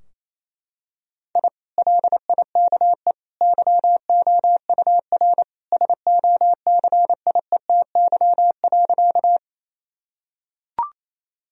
Morse code, heard as a keyed tone, about 28 wpm, 700 Hz. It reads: I LIKE YOUR SOCIETY.
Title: Sentences from Top 1000 Words - Review Part 4 - 28wpm